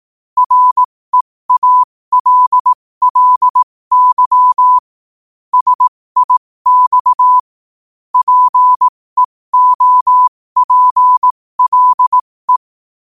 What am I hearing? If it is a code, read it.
Morse code, heard as a keyed tone, about 18 wpm, spaced as for 15 wpm, 1000 Hz. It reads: REALLY SIX PEOPLE
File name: qotdfiles / 15wpm